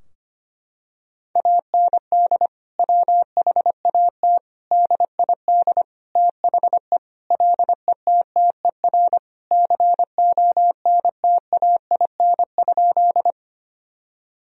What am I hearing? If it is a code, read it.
Morse code, heard as a keyed tone, about 25 wpm, 700 Hz. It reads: AND WHAT DID THE LETTER CONTAIN?
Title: Sentences from Top 1000 Words - Review Part 2 - 25wpm